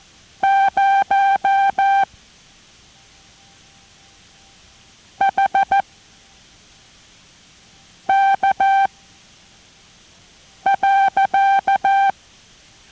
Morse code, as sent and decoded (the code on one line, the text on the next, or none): ----- / .... / -.- / .-.-.-
0 H K .